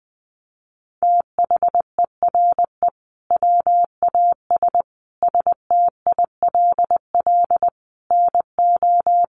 Morse code, heard as a keyed tone, about 20 wpm, 700 Hz. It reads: THERE WAS STILL NO
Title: Sentences from Top 1400 Words 20wpm